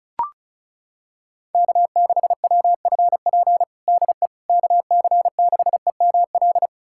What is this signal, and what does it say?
Morse code, heard as a keyed tone, about 36 wpm, 700 Hz. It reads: K6WFP DE KC6EML
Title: USA Callsign DE Callsign 35wpm